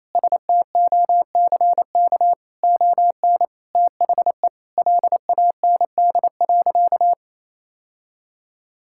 Morse code, heard as a keyed tone, about 28 wpm, 700 Hz. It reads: STOCK ON THE LAND.
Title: Sentences from Top 1000 Words - Review Part 5 - 28wpm